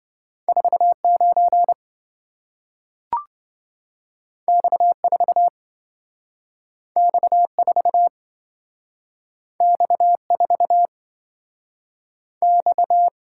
Morse code, heard as a keyed tone, about 27 wpm, 700 Hz. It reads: 49 X4 X4 X4 X